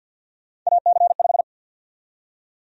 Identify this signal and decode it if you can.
Morse code, heard as a keyed tone, about 52 wpm, 700 Hz. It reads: AC5
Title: USA Call Sign Prefixes - 50wpm